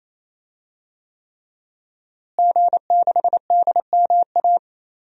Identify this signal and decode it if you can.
Morse code, heard as a keyed tone, about 28 wpm, 700 Hz. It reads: G6DMA
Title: DXCC-100 Call Signs - 5 Characters - Repeated 3x - 28wpm